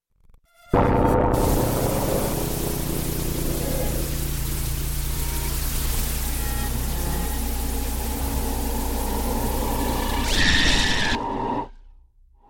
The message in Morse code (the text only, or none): none